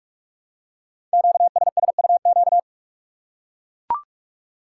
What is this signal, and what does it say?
Morse code, heard as a keyed tone, about 44 wpm, 700 Hz. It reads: QSSUQ
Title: Code Groups - 5 Letters - 45wpm